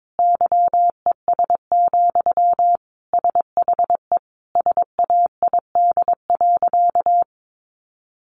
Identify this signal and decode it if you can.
Morse code, heard as a keyed tone, about 22 wpm, 700 Hz. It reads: YES, SHE SAID.
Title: Sentences from Top 1300 Words 22wpm